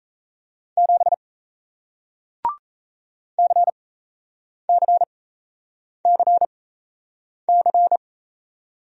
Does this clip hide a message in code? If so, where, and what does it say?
Yes: Morse code from 0.8 to 8.0 s, heard as a keyed tone, about 38 wpm, spaced as for 8 wpm, 700 Hz. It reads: 7 CCCC